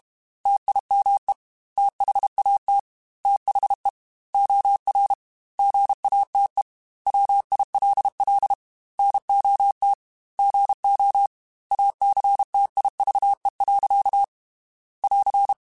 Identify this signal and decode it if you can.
Morse code, heard as a keyed tone, about 32 wpm, 800 Hz. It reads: TIME THAT THE OR GATE WILL NOT GO ACTIVE. <AR>
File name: Technical Subjects / Technical Subjects-mp3s / OR Gate @30